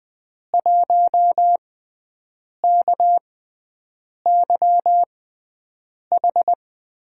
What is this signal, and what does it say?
Morse code, heard as a keyed tone, about 20 wpm, spaced as for 7 wpm, 700 Hz. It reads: 1KYH
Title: USA Call Signs - 6 Characters - 6x Character Spacing - 20wpm